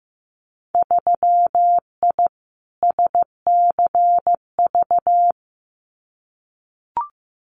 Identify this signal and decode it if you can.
Morse code, heard as a keyed tone, about 15 wpm, 700 Hz. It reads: 3I SCV
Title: Winter Field Day Exchanges Encore 15wpm